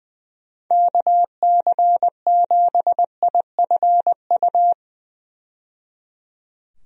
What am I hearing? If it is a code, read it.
Morse code, heard as a keyed tone, about 20 wpm, 700 Hz. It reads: KC7IFU
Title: USA Call Signs - 6 Characters - 20wpm